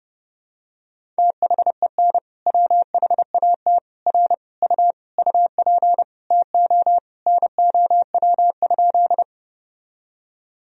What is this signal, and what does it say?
Morse code, heard as a keyed tone, about 32 wpm, 700 Hz. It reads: THEN WHAT R U UP TO NOW?
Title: Sentences from Top 100 Words and Abbreviations 30wpm